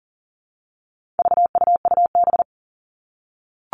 Morse code, heard as a keyed tone, about 37 wpm, 700 Hz. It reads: VUUB
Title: All Cut Numbers - 4-Digit 40wpm